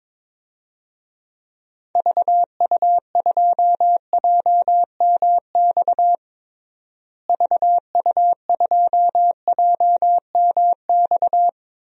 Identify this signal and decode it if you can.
Morse code, heard as a keyed tone, about 22 wpm, 700 Hz. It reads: VU2JMX VU2JMX
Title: DXCC-100 Call Signs - 6 Characters - Repeated 3x - 22wpm